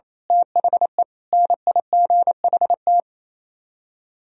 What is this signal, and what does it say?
Morse code, heard as a keyed tone, about 27 wpm, 700 Hz. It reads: THE NIGHT
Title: Speed-Racing - Sets of 5 Words 28wpm